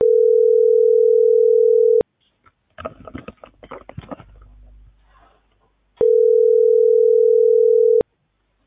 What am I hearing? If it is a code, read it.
Ringback tone (North America), 440 + 480 Hz.